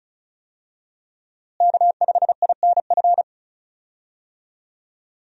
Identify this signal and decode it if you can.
Morse code, heard as a keyed tone, about 36 wpm, 700 Hz. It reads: K5INF
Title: USA Call Signs - 5 Characters - Repeated 3x - 35wpm